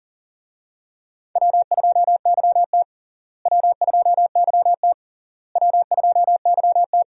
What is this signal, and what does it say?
Morse code, heard as a keyed tone, about 40 wpm, 700 Hz. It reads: W2YT W2YT W2YT
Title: USA Call Signs - 4 Characters - Repeated 3x - 40wpm